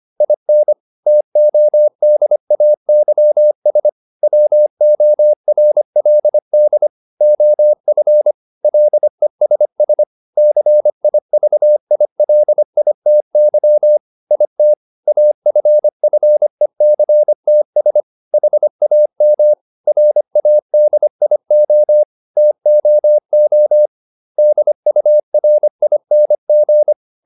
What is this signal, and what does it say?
Morse code, heard as a keyed tone, about 25 wpm, 600 Hz. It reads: IN TODAYS WORLD OF LESS CIVILITY IT AFFECTS HAM RADIO TOO DURING